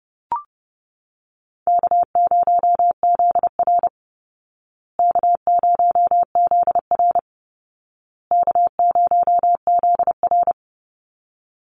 Morse code, heard as a keyed tone, about 29 wpm, 700 Hz. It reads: K0ZR K0ZR K0ZR
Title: Easy Call Signs Repeated 3x - 30wpm